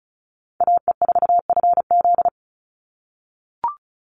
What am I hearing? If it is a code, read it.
Morse code, heard as a keyed tone, about 34 wpm, 700 Hz. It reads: AE4FZ
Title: NA SOTA Chaser Callsigns 35wpm